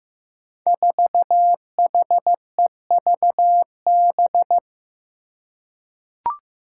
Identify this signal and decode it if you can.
Morse code, heard as a keyed tone, about 15 wpm, 700 Hz. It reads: 4HEVB
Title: Five Challenging Letter-Number Combinations 15wpm